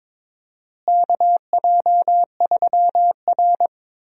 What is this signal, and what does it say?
Morse code, heard as a keyed tone, about 22 wpm, 700 Hz. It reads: KJ3R
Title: USA Call Signs - 4 Characters - Repeated 3x - 22wpm